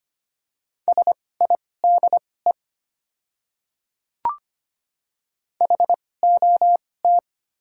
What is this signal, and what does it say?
Morse code, heard as a keyed tone, about 25 wpm, spaced as for 18 wpm, 700 Hz. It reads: SIDE HOT
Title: 2x Character Spacing - Top 100 Words - 25wpm